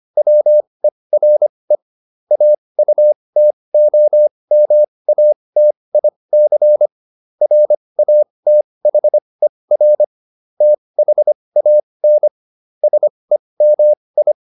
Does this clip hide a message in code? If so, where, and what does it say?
Yes: Morse code from 0.2 to 14.3 s, heard as a keyed tone, about 25 wpm, spaced as for 20 wpm, 600 Hz. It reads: WERE AUTOMATIC RATHER THAN SEMI